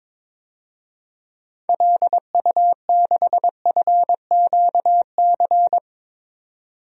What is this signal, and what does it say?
Morse code, heard as a keyed tone, about 21 wpm, 700 Hz. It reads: LU6FQC
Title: DXCC-100 Call Signs - 6 Characters - 22wpm